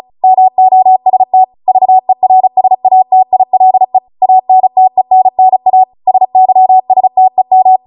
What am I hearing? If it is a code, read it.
Morse code, heard as a keyed tone, about 36 wpm, 750 Hz. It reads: MOST VERSATILE ANTENNA SYSTEM